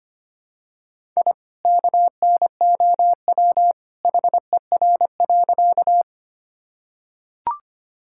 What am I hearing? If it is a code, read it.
Morse code, heard as a keyed tone, about 25 wpm, 700 Hz. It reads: I KNOW HER.
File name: practice / 25wpm / Sentences with 2-Word Phrases - 25wpm